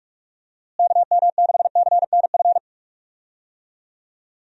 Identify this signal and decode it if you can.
Morse code, heard as a keyed tone, about 44 wpm, 700 Hz. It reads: KM6CNF
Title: USA Call Signs - 6 Characters - Repeated 2x - 45wpm